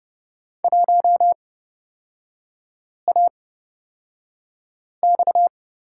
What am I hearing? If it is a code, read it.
Morse code, heard as a keyed tone, about 30 wpm, spaced as for 5 wpm, 700 Hz. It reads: 1AX
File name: Lesson 38e - Letter q - Three Characters 30wpm